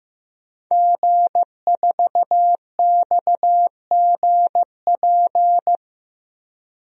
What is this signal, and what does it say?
Morse code, heard as a keyed tone, about 15 wpm, 700 Hz. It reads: G4XGP
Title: DXCC-100 Call Signs - 5 Characters - 15wpm